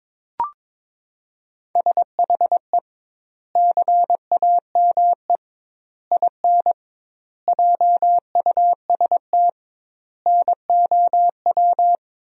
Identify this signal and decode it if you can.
Morse code, heard as a keyed tone, about 22 wpm, 700 Hz. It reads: SHE CAME IN JUST NOW
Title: 2x Word-Spacing - Sentences from Top 300 Words - 22wpm